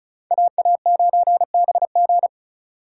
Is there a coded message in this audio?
Yes: Morse code from 0.3 to 2.3 s, heard as a keyed tone, about 46 wpm, 700 Hz. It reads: AA9BG